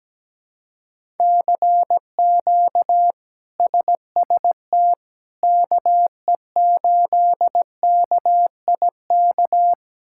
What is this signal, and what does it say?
Morse code, heard as a keyed tone, about 17 wpm, 700 Hz. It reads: CQ SST KE8KIK
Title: K1USN SST QSO Practice - 17wpm